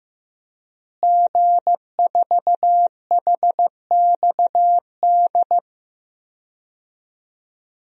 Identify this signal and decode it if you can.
Morse code, heard as a keyed tone, about 15 wpm, 700 Hz. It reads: G4HXD
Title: DXCC-100 Call Signs - 5 Characters - Repeated 2x - 15wpm